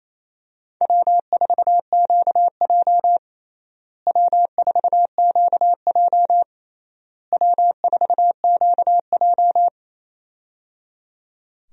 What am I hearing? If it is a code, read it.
Morse code, heard as a keyed tone, about 28 wpm, 700 Hz. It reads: W4QJ W4QJ W4QJ
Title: USA Call Signs - 4 Characters - Repeated 3x - 28wpm